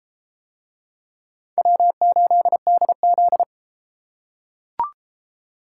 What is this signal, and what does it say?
Morse code, heard as a keyed tone, about 32 wpm, 700 Hz. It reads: W8DZ